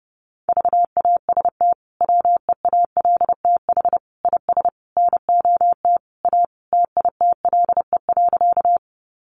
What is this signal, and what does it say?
Morse code, heard as a keyed tone, about 30 wpm, 700 Hz. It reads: VAST WEALTH IS NOT A TITLE.
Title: Sentences from Top 1300 Words 30wpm